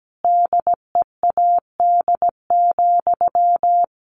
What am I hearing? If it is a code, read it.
Morse code, heard as a keyed tone, about 17 wpm, 700 Hz. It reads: DEAD,
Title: A Sound Of Thunder 17wpm